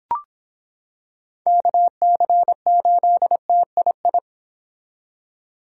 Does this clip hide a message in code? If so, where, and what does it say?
Yes: Morse code from 1.5 to 4.2 s, heard as a keyed tone, about 26 wpm, 700 Hz. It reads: KC8TII